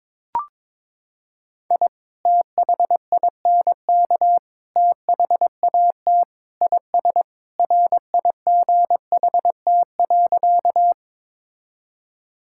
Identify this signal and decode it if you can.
Morse code, heard as a keyed tone, about 22 wpm, 700 Hz. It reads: I THINK THAT IS RIGHT.
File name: practice / 22wpm / Sentences with 2-Word Phrases - 22wpm